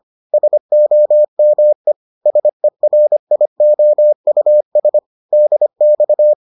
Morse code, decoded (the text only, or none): SOME SERIOUS DX